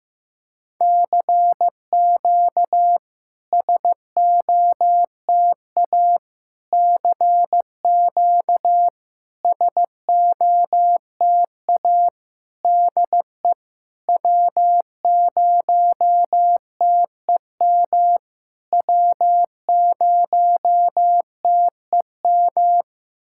Morse code, heard as a keyed tone, about 15 wpm, 700 Hz. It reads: CQ SOTA CQ SOTA DE W0TEM W0TEM
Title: SOTA QSO Practice 15wpm